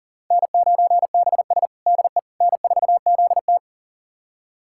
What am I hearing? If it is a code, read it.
Morse code, heard as a keyed tone, about 40 wpm, 700 Hz. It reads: N9BS DE N4ZT
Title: USA Callsign DE Callsign - 4 Characters - 40wpm